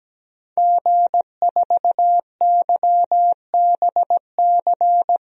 Morse code, heard as a keyed tone, about 17 wpm, 700 Hz. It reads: G4YBC
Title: DXCC-100 Call Signs - 5 Characters - 17wpm